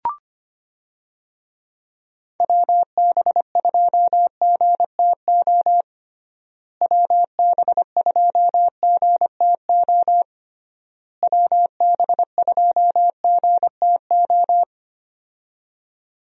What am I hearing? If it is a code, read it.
Morse code, heard as a keyed tone, about 25 wpm, 700 Hz. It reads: WB2GTO WB2GTO WB2GTO
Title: USA Call Signs - 6 Characters - Repeated 3x - 25wpm